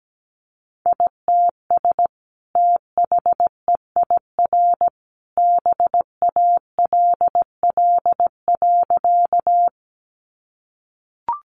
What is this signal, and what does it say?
Morse code, heard as a keyed tone, about 17 wpm, 700 Hz. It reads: ITS THEIR BALL.
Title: Sentences from Top 900 Words 17wpm